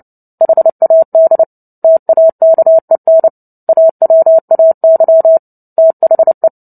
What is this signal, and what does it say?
Morse code, heard as a keyed tone, about 30 wpm, 650 Hz. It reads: HAD TAKEN AWAY THE
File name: Book Two / 03-Chapter 3 - 29 WPM